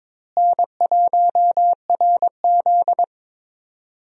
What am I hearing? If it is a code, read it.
Morse code, heard as a keyed tone, about 22 wpm, 700 Hz. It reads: N1RZ
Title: USA Call Signs - 4 Characters - Repeated 3x - 22wpm